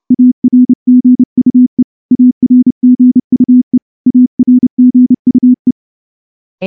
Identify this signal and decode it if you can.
Morse code, heard as a keyed tone, about 28 wpm, 250 Hz. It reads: ARGUE ARGUE ARGUE
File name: morse-training-30WPM